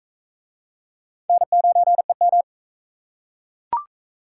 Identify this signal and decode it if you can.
Morse code, heard as a keyed tone, about 40 wpm, 700 Hz. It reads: N9EM